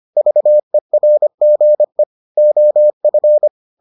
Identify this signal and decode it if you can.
Morse code, heard as a keyed tone, about 25 wpm, 600 Hz. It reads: VERGE OF